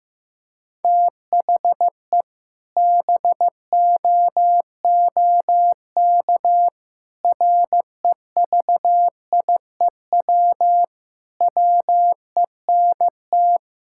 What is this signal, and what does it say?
Morse code, heard as a keyed tone, about 15 wpm, 700 Hz. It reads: THE BOOK REVIEW WENT
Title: Sentences from Top 1000 Words - Review Part 6 - 15wpm